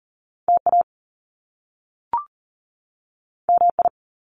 Morse code, heard as a keyed tone, about 40 wpm, 700 Hz. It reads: TA MI
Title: Top Two Letter Combinations in Words 40wpm